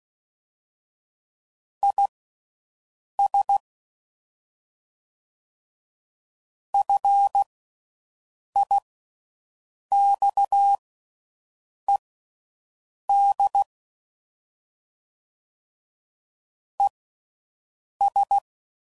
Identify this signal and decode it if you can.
Morse code, heard as a keyed tone, about 16 wpm, spaced as for 6 wpm, 800 Hz. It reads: IS FIXED ES